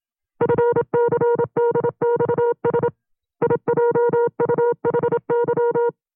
Morse code, heard as a keyed tone, about 27 wpm, 470 Hz. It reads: FCDXS IJUHY
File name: CW-practice/25WPM